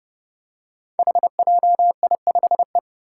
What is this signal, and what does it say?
Morse code, heard as a keyed tone, about 30 wpm, 700 Hz. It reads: HJI5E
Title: Five Challenging Letter-Number Combinations 30wpm